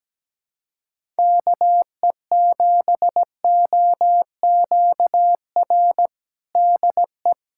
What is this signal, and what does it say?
Morse code, heard as a keyed tone, about 17 wpm, 700 Hz. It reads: KE7OQR DE
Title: USA Callsign DE Callsign - 6 Characters - 17wpm